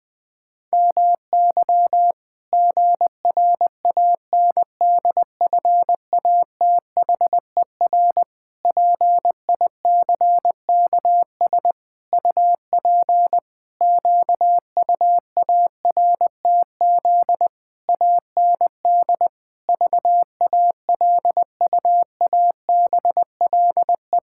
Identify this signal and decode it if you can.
Morse code, heard as a keyed tone, about 20 wpm, 700 Hz. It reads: MY GRANDFATHER PICKS UP QUARTZ AND VALUABLE
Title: Pangram Challenge v2 - 20wpm